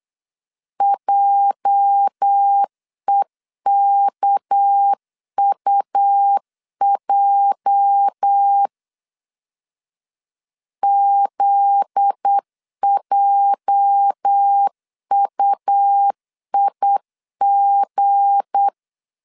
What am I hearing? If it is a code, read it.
Morse code, heard as a keyed tone, about 8 wpm, 800 Hz. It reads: JEKUJ ZJUIG
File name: LEZIONI / 8 YJ45 / 8 YJ 8wpm 3313